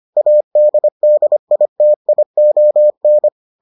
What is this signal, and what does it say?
Morse code, heard as a keyed tone, about 25 wpm, 600 Hz. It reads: ADDITION